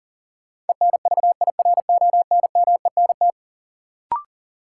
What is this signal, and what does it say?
Morse code, heard as a keyed tone, about 40 wpm, 700 Hz. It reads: ENVIRONMENT
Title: Sentences from Top 1000 Words - Review Part 5 - New Word Drill - 40wpm